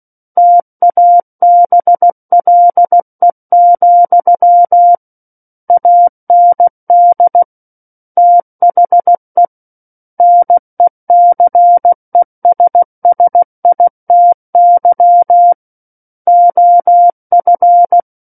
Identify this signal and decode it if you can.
Morse code, heard as a keyed tone, about 16 wpm, 700 Hz. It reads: TABLE, AND THE NECESSITY OF
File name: Book One / 07-Chapter 7 - 16 WPM